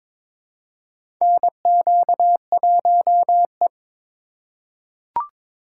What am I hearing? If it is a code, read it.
Morse code, heard as a keyed tone, about 22 wpm, 700 Hz. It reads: NQ1E
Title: USA Call Signs - 4 Characters - 3x Character Spacing - 22wpm